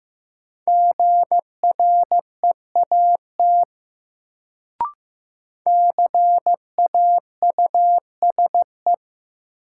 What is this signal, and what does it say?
Morse code, heard as a keyed tone, about 15 wpm, 700 Hz. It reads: GREAT CAUSE
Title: Sentences from Top 200 Words - New Word Drill - 15wpm